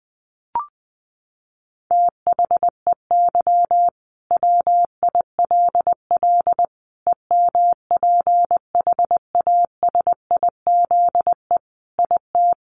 Morse code, heard as a keyed tone, about 20 wpm, 700 Hz. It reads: THEY WILL EMPHASIZE IT
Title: Sentences from Top 1500 Words 20wpm